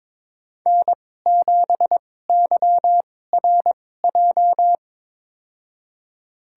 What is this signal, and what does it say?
Morse code, heard as a keyed tone, about 21 wpm, spaced as for 16 wpm, 700 Hz. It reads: N7YRJ
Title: USA Call Signs - 5 Characters - 2x Character Spacing - 22wpm